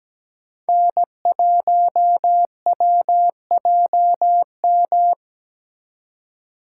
Morse code, heard as a keyed tone, about 20 wpm, 700 Hz. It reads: N1WJM
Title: Speed Racing - USA Call Signs 17wpm